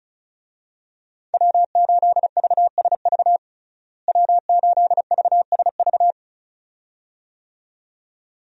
Morse code, heard as a keyed tone, about 34 wpm, 700 Hz. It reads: W8VSV W8VSV
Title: USA Call Signs - 5 Characters - Repeated 2x - 35wpm